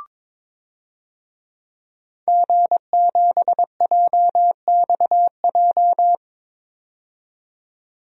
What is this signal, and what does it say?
Morse code, heard as a keyed tone, about 22 wpm, 700 Hz. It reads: G7JXJ
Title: DXCC-100 Call Signs - 5 Characters - 22wpm